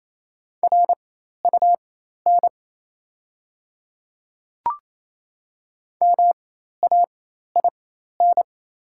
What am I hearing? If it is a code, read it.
Morse code, heard as a keyed tone, about 28 wpm, spaced as for 13 wpm, 700 Hz. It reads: RUN MAIN
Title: 4x Character Spacing - Top 300 Words - 28wpm